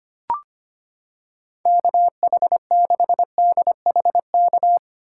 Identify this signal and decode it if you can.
Morse code, heard as a keyed tone, about 25 wpm, 700 Hz. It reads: KH6DHK
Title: K1USN SST QSO Practice - 25wpm